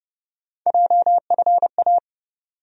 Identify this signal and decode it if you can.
Morse code, heard as a keyed tone, about 30 wpm, 700 Hz. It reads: JFA
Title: USA Call Sign Postfixes - 3 Letters 30wpm